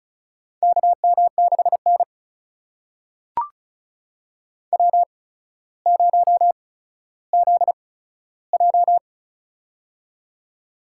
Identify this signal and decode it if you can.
Morse code, heard as a keyed tone, about 36 wpm, spaced as for 15 wpm, 700 Hz. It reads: KM6N W0ZJ
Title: USA Call Signs - 4 Characters - 8x Character Spacing - 35wpm